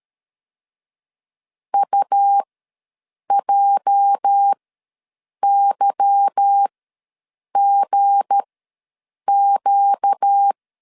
Morse code, heard as a keyed tone, about 13 wpm, spaced as for 7 wpm, 800 Hz. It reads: UJYGQ
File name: LEZIONI / 8 YJ45 / 8 YJ ws 12wpm 3342